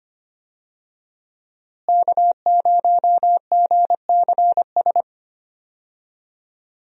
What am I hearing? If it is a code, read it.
Morse code, heard as a keyed tone, about 25 wpm, 700 Hz. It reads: K0GCS
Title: USA Call Signs - 5 Characters - Repeated 2x - 25wpm